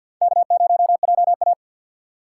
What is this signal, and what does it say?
Morse code, heard as a keyed tone, about 54 wpm, 700 Hz. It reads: K0JA
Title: USA Call Signs - 6x Character Spacing - 50wpm